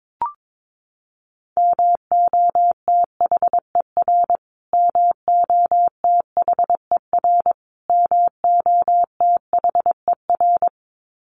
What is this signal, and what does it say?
Morse code, heard as a keyed tone, about 22 wpm, 700 Hz. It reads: MOTHER MOTHER MOTHER
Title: Top 500 Words Repeated 22wpm